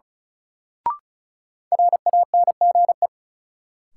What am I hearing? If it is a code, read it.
Morse code, heard as a keyed tone, about 36 wpm, 700 Hz. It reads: RANGE
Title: Sets of 4 Words - New Word Drill - 35wpm